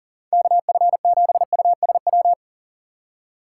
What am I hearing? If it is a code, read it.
Morse code, heard as a keyed tone, about 40 wpm, 700 Hz. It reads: KF7USW
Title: USA Call Signs - 6 Characters - 3x Character Spacing - 40wpm